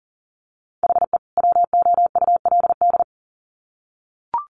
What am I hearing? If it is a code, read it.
Morse code, heard as a keyed tone, about 40 wpm, 700 Hz. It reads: HE WOULD
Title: Sets of 2 Words 40wpm